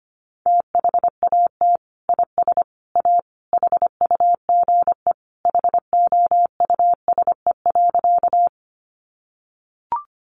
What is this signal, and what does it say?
Morse code, heard as a keyed tone, about 25 wpm, 700 Hz. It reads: THAT IS A HUGE HOUSE.
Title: Sentences from Top 1100 Words 25wpm